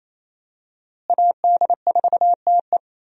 Morse code, heard as a keyed tone, about 28 wpm, 700 Hz. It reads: AD4TE
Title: USA Call Signs - 5 Characters - Repeated 2x - 28wpm